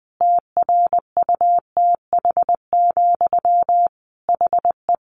Morse code, heard as a keyed tone, about 20 wpm, 700 Hz. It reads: TRUTH, HE
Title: Sentences from Top 1500 Words 20wpm